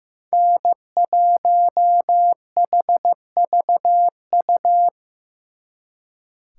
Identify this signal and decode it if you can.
Morse code, heard as a keyed tone, about 18 wpm, 700 Hz. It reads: N1HVU
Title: Speed Racing - USA 1x3 Call Signs 15wpm